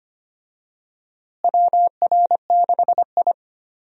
Morse code, heard as a keyed tone, about 25 wpm, 700 Hz. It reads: WR6I
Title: USA Call Signs - 4 Characters - Repeated 2x - 25wpm